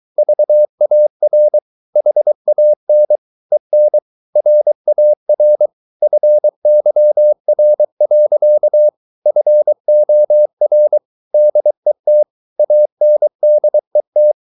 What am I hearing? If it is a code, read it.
Morse code, heard as a keyed tone, about 23 wpm, 600 Hz. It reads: VAR HAN EN RAR FYR. FOR DET ANDET
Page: Ferie 23 wpm